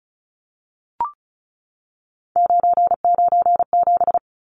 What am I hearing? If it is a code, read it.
Morse code, heard as a keyed tone, about 34 wpm, 700 Hz. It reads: 997
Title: Numbers - 3-Digit 35wpm